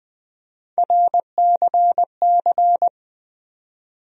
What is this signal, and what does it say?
Morse code, heard as a keyed tone, about 20 wpm, 700 Hz. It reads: RCC
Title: USA Call Sign Postfixes - 3 Letters 20wpm